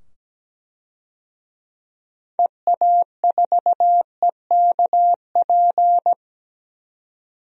Morse code, heard as a keyed tone, about 17 wpm, 700 Hz. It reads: EA4EKP
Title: DXCC-100 Call Signs - 6 Characters - 17wpm